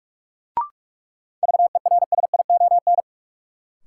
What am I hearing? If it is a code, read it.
Morse code, heard as a keyed tone, about 46 wpm, 700 Hz. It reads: VERSION
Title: Sentences from Top 1400 Words - New Word Drill - 45wpm